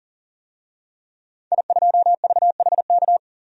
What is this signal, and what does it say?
Morse code, heard as a keyed tone, about 40 wpm, 700 Hz. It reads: I2VHK
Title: DXCC-100 Call Signs - 5 Characters - Repeated 3x - 40wpm